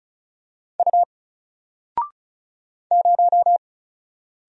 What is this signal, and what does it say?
Morse code, heard as a keyed tone, about 36 wpm, 700 Hz. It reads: U 0
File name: Single-Letter-Number-35wpm